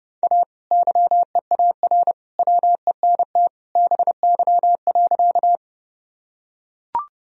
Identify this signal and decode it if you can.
Morse code, heard as a keyed tone, about 30 wpm, 700 Hz. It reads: A YEAR WENT BY.
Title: Sentences from Top 1000 Words - Review Part 1 - 30wpm